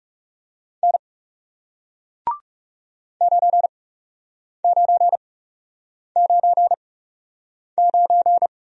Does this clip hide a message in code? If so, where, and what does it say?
Yes: Morse code from 0.8 to 8.5 s, heard as a keyed tone, about 40 wpm, spaced as for 8 wpm, 700 Hz. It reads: N 9999